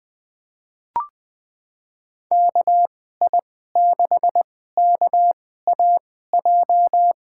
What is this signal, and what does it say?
Morse code, heard as a keyed tone, about 21 wpm, spaced as for 15 wpm, 700 Hz. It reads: KI6KAJ
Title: USA Call Signs - 6 Characters - 2x Character Spacing - 20wpm